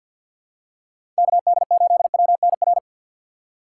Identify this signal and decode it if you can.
Morse code, heard as a keyed tone, about 51 wpm, 700 Hz. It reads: KD8WNR
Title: USA Call Signs - 6 Characters - 3x Character Spacing - 50wpm